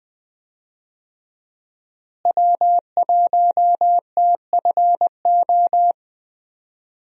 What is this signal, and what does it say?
Morse code, heard as a keyed tone, about 20 wpm, 700 Hz. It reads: W1TFO
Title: USA Call Signs - 5 Characters - Repeated 2x - 20wpm